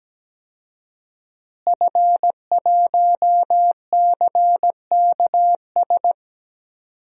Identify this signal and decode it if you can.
Morse code, heard as a keyed tone, about 17 wpm, 700 Hz. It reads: F1CKS
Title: DXCC-100 Call Signs - 5 Characters - Repeated 3x - 17wpm